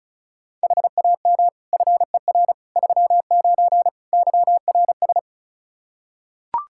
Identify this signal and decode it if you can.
Morse code, heard as a keyed tone, about 34 wpm, 700 Hz. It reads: HAM FER 39 YRS
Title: Practice QSO - Exchange 3 - More Info - 35wpm